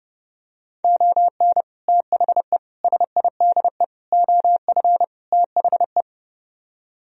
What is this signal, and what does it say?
Morse code, heard as a keyed tone, about 30 wpm, 700 Hz. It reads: ON THE SIDE OF THE